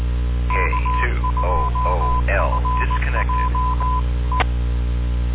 Morse code, heard as a keyed tone, about 18 wpm, 1010 Hz, with noise. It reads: K2FJ E